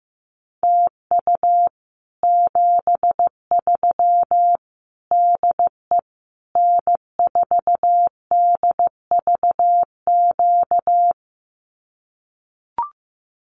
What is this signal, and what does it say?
Morse code, heard as a keyed tone, about 15 wpm, 700 Hz. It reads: TU 73 DE N4DVQ
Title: Common QSO Phrases Encore 15wpm